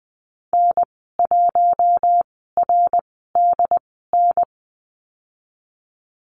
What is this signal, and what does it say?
Morse code, heard as a keyed tone, about 20 wpm, spaced as for 15 wpm, 700 Hz. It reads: N1RDN
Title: 069 Lesson 12g - Letter d - Call Signs 20wpm